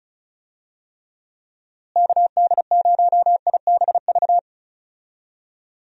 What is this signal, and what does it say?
Morse code, heard as a keyed tone, about 36 wpm, 700 Hz. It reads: KD0IBV
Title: USA Call Signs - 6 Characters - Repeated 2x - 35wpm